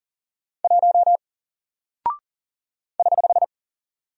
Morse code, heard as a keyed tone, about 40 wpm, 700 Hz. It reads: 1 <HH>
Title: Single Character (Letter:Number:Punctuation:Prosign) Alt 40wpm